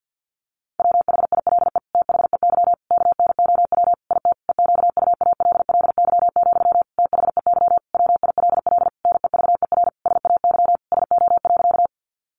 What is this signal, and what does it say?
Morse code, heard as a keyed tone, about 50 wpm, 700 Hz. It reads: WHILE THEY KNOW IT EQUALLY, THEY WILL NEVER SAY SO.
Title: Sentences from Top 1400 Words 50wpm